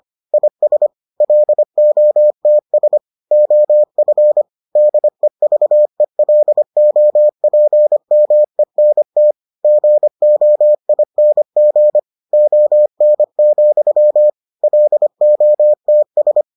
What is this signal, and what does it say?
Morse code, heard as a keyed tone, about 25 wpm, 600 Hz. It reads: IS LOTS OF DEVELOPMENT GOING ON, LOTS